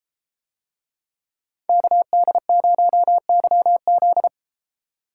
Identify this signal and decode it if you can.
Morse code, heard as a keyed tone, about 33 wpm, 700 Hz. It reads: KD0YZ